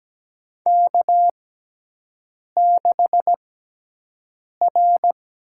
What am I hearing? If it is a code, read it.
Morse code, heard as a keyed tone, about 17 wpm, spaced as for 6 wpm, 700 Hz. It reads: K6R
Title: USA Call Signs - 6x Character Spacing - 17wpm